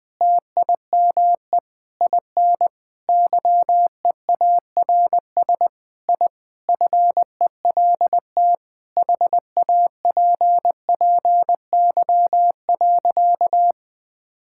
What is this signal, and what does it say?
Morse code, heard as a keyed tone, about 21 wpm, 700 Hz. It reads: TIME IN YEARS I FELT HAPPY.
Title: Sentences with 5-Word Phrases - 20wpm